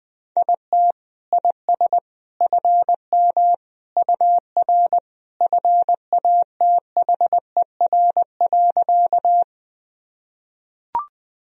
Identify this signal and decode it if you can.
Morse code, heard as a keyed tone, about 19 wpm, 700 Hz. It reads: IT IS FM UR FATHER.
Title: Sentences from Top 300 Words and Abbreviations 20wpm